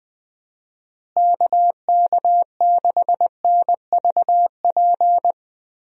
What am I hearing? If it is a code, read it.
Morse code, heard as a keyed tone, about 20 wpm, 700 Hz. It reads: KK6NVP
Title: USA Call Signs - 6 Characters - Repeated 3x - 20wpm